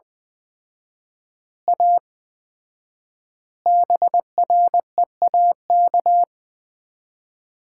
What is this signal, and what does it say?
Morse code, heard as a keyed tone, about 20 wpm, 700 Hz. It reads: A BREAK
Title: 4x Word-Spacing - Sets of 4 Words - 20wpm